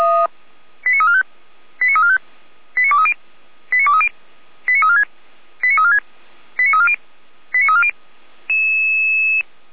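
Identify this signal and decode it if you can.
Five-tone selective call (ZVEI), heard as quick runs of five tones, tones of about 70 ms. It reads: 89345 89345 89230 89230 89346 89346 89340 89340
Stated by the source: ZVEI five-tone call received by SDR.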